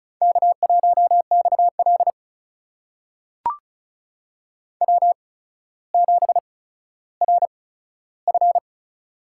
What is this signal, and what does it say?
Morse code, heard as a keyed tone, about 36 wpm, spaced as for 15 wpm, 700 Hz. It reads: K1XL W7RF
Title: USA Call Signs - 4 Characters - 8x Character Spacing - 35wpm